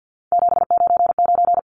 Morse code, heard as a keyed tone, about 50 wpm, 700 Hz. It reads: 799
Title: Numbers - 3-Digit 50wpm